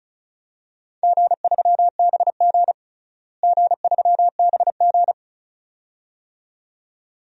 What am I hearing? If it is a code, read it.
Morse code, heard as a keyed tone, about 36 wpm, 700 Hz. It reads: G3BG G3BG
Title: DXCC-100 Call Signs - 4 Characters - Repeated 2x - 35wpm